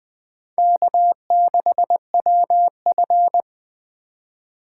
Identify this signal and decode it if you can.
Morse code, heard as a keyed tone, about 23 wpm, 700 Hz. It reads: K6WF
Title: Speed Racing - USA 1x2 Call Signs 20wpm